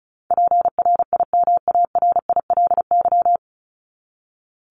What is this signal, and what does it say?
Morse code, heard as a keyed tone, about 34 wpm, 700 Hz. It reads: PRIMARILY
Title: Top 1001-1250 Words 35wpm